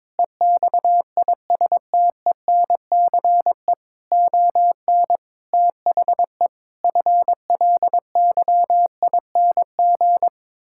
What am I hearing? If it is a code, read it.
Morse code, heard as a keyed tone, about 22 wpm, 700 Hz. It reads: EXISTENCE ON THE FLYING
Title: Pangram Challenge v2 - 22wpm